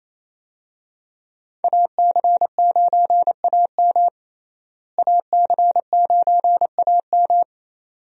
Morse code, heard as a keyed tone, about 28 wpm, 700 Hz. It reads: AC9AM AC9AM
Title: USA Call Signs - 5 Characters - Repeated 2x - 28wpm